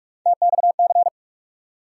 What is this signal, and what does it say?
Morse code, heard as a keyed tone, about 42 wpm, 700 Hz. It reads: TXC
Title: USA Call Sign Postfixes - 3 Letters 45wpm